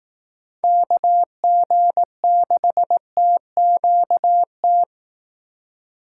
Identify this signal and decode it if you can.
Morse code, heard as a keyed tone, about 18 wpm, 700 Hz. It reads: KG6TQT
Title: Speed Racing - USA 2x3 Call Signs 15wpm